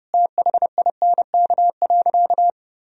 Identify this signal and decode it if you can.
Morse code, heard as a keyed tone, about 30 wpm, 700 Hz. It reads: THINK.
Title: Sentences from Top 1000 Words - Review Part 2 - 30wpm